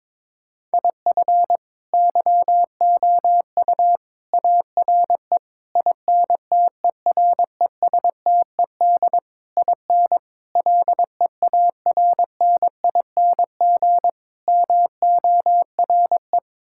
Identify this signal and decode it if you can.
Morse code, heard as a keyed tone, about 22 wpm, 700 Hz. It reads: IF YOU ARE INTERESTED IN LEARNING MORE
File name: practice / 22wpm / Sets of 8 Words - 22wpm